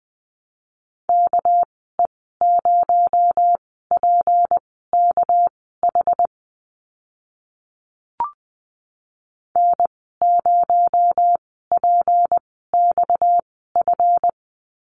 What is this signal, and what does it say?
Morse code, heard as a keyed tone, about 20 wpm, spaced as for 15 wpm, 700 Hz. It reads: KE0PKH N0PXF
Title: 231 Lesson 37g - Letter x - Call Signs 20wpm